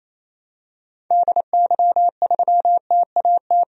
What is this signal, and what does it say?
Morse code, heard as a keyed tone, about 28 wpm, 700 Hz. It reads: DY3TAT
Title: DXCC-100 Call Signs - 6 Characters - 28wpm